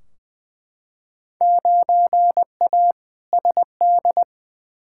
Morse code, heard as a keyed tone, about 20 wpm, 700 Hz. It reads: 9A SD
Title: Field Day Exchanges Encore - 20wpm